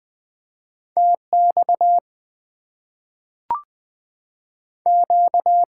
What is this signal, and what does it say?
Morse code, heard as a keyed tone, about 20 wpm, 690 Hz. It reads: TX Q
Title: Top 90 QSO Elements - 3x Character Spacing - 20wpm